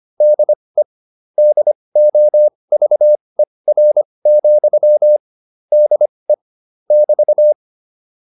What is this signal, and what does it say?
Morse code, heard as a keyed tone, about 25 wpm, spaced as for 20 wpm, 600 Hz. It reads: DE DOVER, DE <BT>